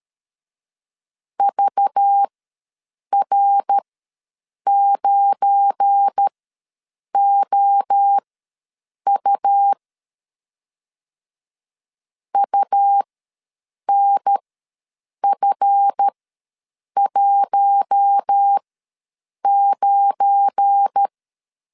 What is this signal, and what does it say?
Morse code, heard as a keyed tone, about 13 wpm, spaced as for 7 wpm, 800 Hz. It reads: VR9OU UNF19
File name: LEZIONI / 7 GF9Z / 7 GF9 ws 12wpm 3344